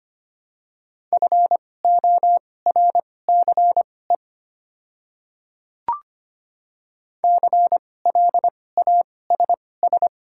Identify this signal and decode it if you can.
Morse code, heard as a keyed tone, about 25 wpm, spaced as for 18 wpm, 700 Hz. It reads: FORCE CLASS
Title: 2x Character Spacing - Top 500 Words - 25wpm